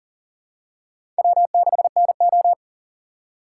Morse code, heard as a keyed tone, about 40 wpm, 700 Hz. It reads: W6NO
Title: USA Call Signs - 4 Characters - 4x Character Spacing - 40wpm